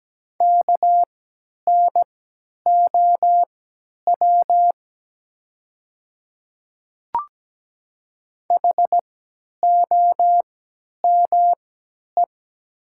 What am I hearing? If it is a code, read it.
Morse code, heard as a keyed tone, about 17 wpm, spaced as for 10 wpm, 700 Hz. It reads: KNOW HOME
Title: 3x Character Spacing - Top 200 Words - 17wpm